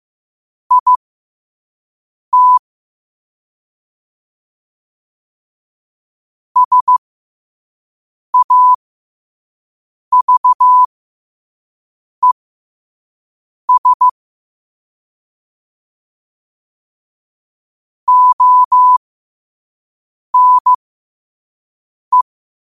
Morse code, heard as a keyed tone, about 15 wpm, spaced as for 5 wpm, 1000 Hz. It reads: IT SAVES ONE
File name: qotdfiles / 5wpm